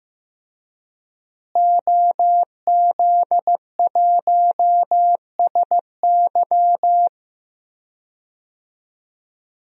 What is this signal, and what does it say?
Morse code, heard as a keyed tone, about 15 wpm, 700 Hz. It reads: OZ1SY
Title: DXCC-100 Call Signs - 5 Characters - Repeated 2x - 15wpm